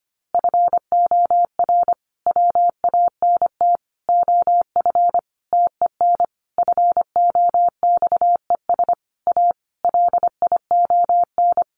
Morse code, heard as a keyed tone, about 25 wpm, 700 Hz. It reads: FOR WANT OF TEN FOXES A LION
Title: A Sound Of Thunder 25wpm